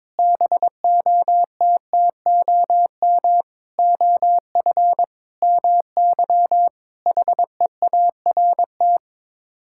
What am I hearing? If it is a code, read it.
Morse code, heard as a keyed tone, about 21 wpm, 700 Hz. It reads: BOTTOM OF MY HEART
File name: practice / 22wpm / Sets of 8 Words - 22wpm